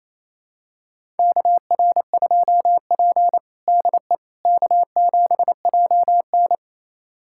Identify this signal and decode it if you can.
Morse code, heard as a keyed tone, about 28 wpm, 700 Hz. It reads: KR2P DE K7JN
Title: USA Callsign DE Callsign - 4 Characters - 28wpm